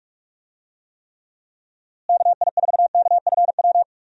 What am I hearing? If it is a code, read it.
Morse code, heard as a keyed tone, about 46 wpm, 700 Hz. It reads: KI4KFW
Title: USA Call Signs - 6 Characters - Repeated 3x - 45wpm